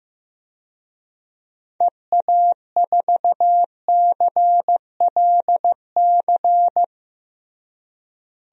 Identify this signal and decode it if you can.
Morse code, heard as a keyed tone, about 15 wpm, 700 Hz. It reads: EA4CLC
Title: DXCC-100 Call Signs - 6 Characters - Repeated 2x - 15wpm